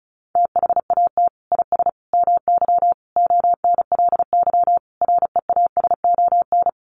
Morse code, heard as a keyed tone, about 34 wpm, 700 Hz. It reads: THAT IS MY ONLY REASON
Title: Sentences from Top 700 Words 35wpm